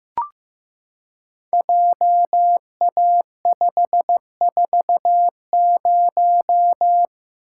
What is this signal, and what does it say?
Morse code, heard as a keyed tone, about 15 wpm, 700 Hz. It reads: JA540
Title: Code Groups - 5 Letter:Numbers - 15wpm